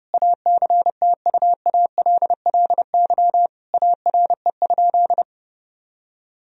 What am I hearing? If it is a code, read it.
Morse code, heard as a keyed tone, about 30 wpm, 700 Hz. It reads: ACTUALLY ARE?
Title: Sentences from Top 1000 Words - Review Part 4 - 30wpm